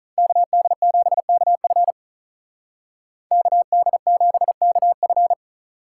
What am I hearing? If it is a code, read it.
Morse code, heard as a keyed tone, about 43 wpm, 700 Hz. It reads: KD7KF KD7KF
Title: Speed Racing - USA Call Signs 35wpm